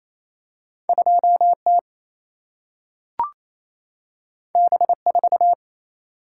Morse code, heard as a keyed tone, about 28 wpm, 700 Hz. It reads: 2T B4
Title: Single-Letter Single-Number 28wpm